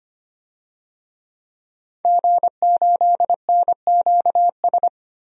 Morse code, heard as a keyed tone, about 25 wpm, 700 Hz. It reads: G8NQS
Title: DXCC-100 Call Signs - 5 Characters - 25wpm